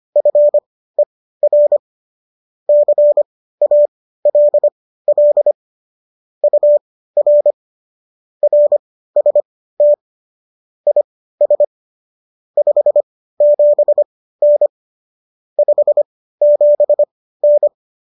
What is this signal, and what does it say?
Morse code, heard as a keyed tone, about 25 wpm, spaced as for 15 wpm, 590 Hz. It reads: FER CALL UR RST IS 57N 57N